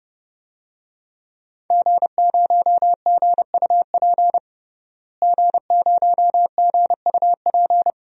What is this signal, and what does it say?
Morse code, heard as a keyed tone, about 30 wpm, 700 Hz. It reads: G0GUP G0GUP
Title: DXCC-100 Call Signs - 5 Characters - Repeated 3x - 30wpm